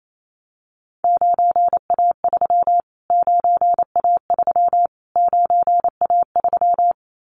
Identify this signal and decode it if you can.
Morse code, heard as a keyed tone, about 28 wpm, 700 Hz. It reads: 9A3 9A3 9A3
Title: Callsign Prefixes for Top 25 DXCC - Repeated 3x - 28wpm